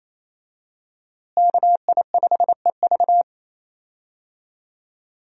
Morse code, heard as a keyed tone, about 28 wpm, 700 Hz. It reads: KI5EV
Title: USA Call Signs - 5 Characters - Repeated 3x - 28wpm